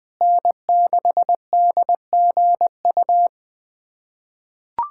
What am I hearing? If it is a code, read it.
Morse code, heard as a keyed tone, about 21 wpm, 700 Hz. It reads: N6DGU
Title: USA Callsign DE Callsign - 5 Characters - 20wpm